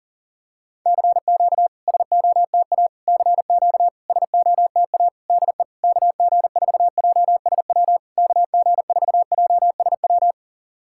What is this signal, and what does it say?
Morse code, heard as a keyed tone, about 40 wpm, 700 Hz. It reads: CQ SOTA CQ SOTA DE KG4JSW KG4JSW
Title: SOTA QSO Practice 40wpm